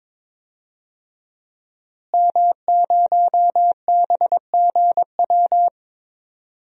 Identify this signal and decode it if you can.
Morse code, heard as a keyed tone, about 22 wpm, 700 Hz. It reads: M0BGW
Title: DXCC-100 Call Signs - 5 Characters - Repeated 2x - 22wpm